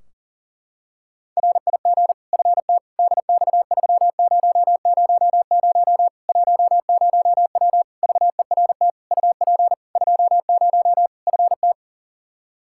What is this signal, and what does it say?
Morse code, heard as a keyed tone, about 40 wpm, 700 Hz. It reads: RIG FT DX3000 10W VERT UP 20 FT
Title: Practice QSO - Exchange 2 - Info - 40wpm